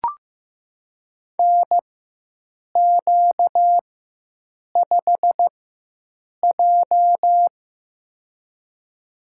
Morse code, heard as a keyed tone, about 16 wpm, spaced as for 7 wpm, 700 Hz. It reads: NQ5J
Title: USA Call Signs - 4x Character Spacing - 15wpm